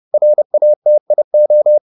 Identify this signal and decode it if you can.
Morse code, heard as a keyed tone, about 30 wpm, 600 Hz. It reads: RATIO